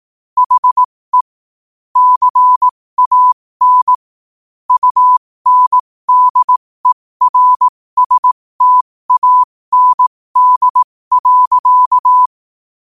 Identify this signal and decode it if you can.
Morse code, heard as a keyed tone, about 18 wpm, spaced as for 15 wpm, 1000 Hz. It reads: HE CAN UNDERSTAND.
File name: qotdfiles / 15wpm